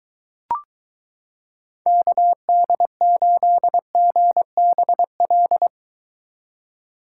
Morse code, heard as a keyed tone, about 22 wpm, 700 Hz. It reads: KD8GBL